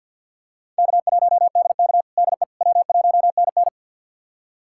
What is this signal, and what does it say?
Morse code, heard as a keyed tone, about 50 wpm, 700 Hz. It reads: K1DK DE W1NN
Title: USA Callsign DE Callsign - 4 Characters - 50wpm